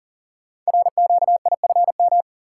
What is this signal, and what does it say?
Morse code, heard as a keyed tone, about 40 wpm, 700 Hz. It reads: RQIFM
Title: Code Groups - 5 Letters - 40wpm